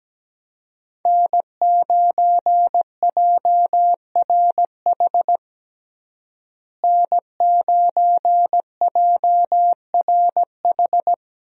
Morse code, heard as a keyed tone, about 17 wpm, 700 Hz. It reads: N9JRH N9JRH
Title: USA Call Signs - 5 Characters - Repeated 2x - 17wpm